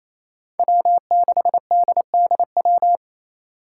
Morse code, heard as a keyed tone, about 28 wpm, 700 Hz. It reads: W6DDW
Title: Five Challenging Letter-Number Combinations 28wpm